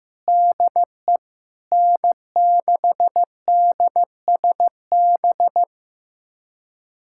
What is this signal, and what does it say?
Morse code, heard as a keyed tone, about 16 wpm, 700 Hz. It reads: DE N6DSB
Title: USA Callsign DE Callsign - 5 Characters - 15wpm